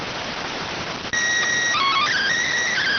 SSTV (slow-scan television), PD 290.